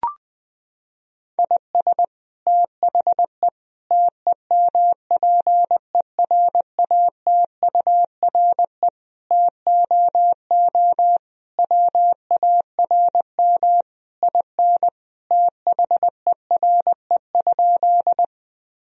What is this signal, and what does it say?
Morse code, heard as a keyed tone, about 20 wpm, 700 Hz. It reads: IS THE TEMPERATURE TOO WARM IN THERE?
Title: Sentences from Top 1000 Words - Review Part 4 - 20wpm